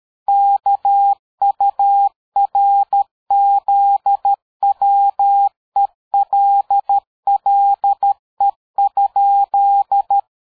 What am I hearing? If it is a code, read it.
Morse code, heard as a keyed tone, about 13 wpm, 800 Hz. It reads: KURZWELLE?
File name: dk5ke / PersWorte 60